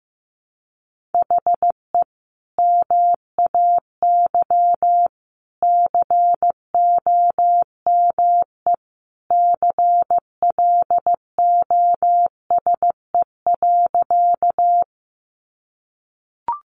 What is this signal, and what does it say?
Morse code, heard as a keyed tone, about 15 wpm, 700 Hz. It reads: HE MAY COME CLOSE.
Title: Sentences from Top 300 Words 15wpm